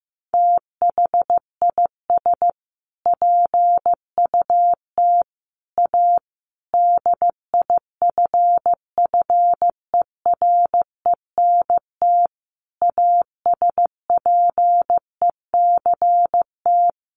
Morse code, heard as a keyed tone, about 14 wpm, 700 Hz. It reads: THIS PUT A DIFFERENT ASPECT
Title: Sentences from Top 1300 Words 15wpm